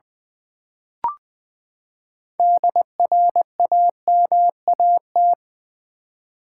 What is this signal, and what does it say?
Morse code, heard as a keyed tone, about 20 wpm, 700 Hz. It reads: DRAMAT
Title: Build A Word - 6 Letters At A Time - 12-Letter Words - 20wpm